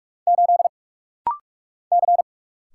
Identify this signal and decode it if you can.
Morse code, heard as a keyed tone, about 46 wpm, 700 Hz. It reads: 8 C
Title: Single Letter-Number Alt - Rapid-Fire 45wpm